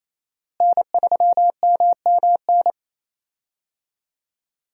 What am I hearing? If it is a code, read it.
Morse code, heard as a keyed tone, about 28 wpm, 700 Hz. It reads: N3MMN
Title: USA Call Signs - 5 Characters - Repeated 3x - 28wpm